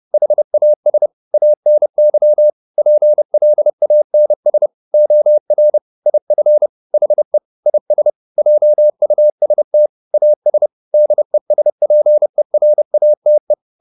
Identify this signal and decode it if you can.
Morse code, heard as a keyed tone, about 30 wpm, 600 Hz. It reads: HAS ANY PLANS OR IF HE IS JUST AS DESPERATE
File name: the old man and the sea / the old man and the sea 1 30wpm